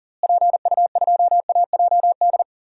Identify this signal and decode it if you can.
Morse code, heard as a keyed tone, about 40 wpm, 700 Hz. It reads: PU2AJD
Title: DXCC-100 Call Signs - 6 Characters - Repeated 3x - 40wpm